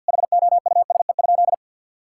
Morse code, heard as a keyed tone, about 50 wpm, 700 Hz. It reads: HOUSE?
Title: Sentences from Top 1000 Words - Review Part 6 - 50wpm